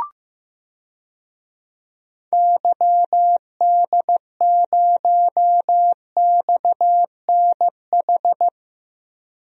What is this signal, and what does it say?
Morse code, heard as a keyed tone, about 15 wpm, 700 Hz. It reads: YD0XNH
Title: DXCC-100 Call Signs - 6 Characters - 15wpm